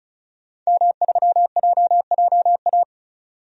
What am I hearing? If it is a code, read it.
Morse code, heard as a keyed tone, about 36 wpm, 700 Hz. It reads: M3JJA
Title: DXCC-100 Call Signs - 5 Characters - 35wpm